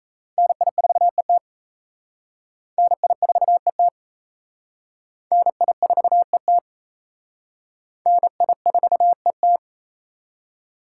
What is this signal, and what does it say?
Morse code, heard as a keyed tone, about 34 wpm, 700 Hz. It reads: NI4ET NI4ET NI4ET NI4ET
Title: Speed Racing - USA 2x2 Call Signs 28wpm